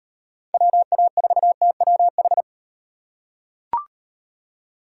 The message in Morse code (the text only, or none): WA4TWH